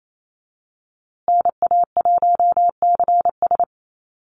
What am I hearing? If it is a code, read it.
Morse code, heard as a keyed tone, about 28 wpm, 700 Hz. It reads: NA1CS
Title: USA Call Signs Repeated 2x 28wpm